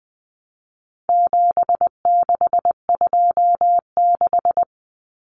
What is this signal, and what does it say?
Morse code, heard as a keyed tone, about 20 wpm, 700 Hz. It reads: 7626
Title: Numbers - 4-Digit 20wpm